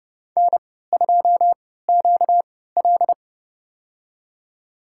Morse code, heard as a keyed tone, about 30 wpm, spaced as for 17 wpm, 700 Hz. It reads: N2QL
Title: USA Call Signs - 4 Characters - 3x Character Spacing - 30wpm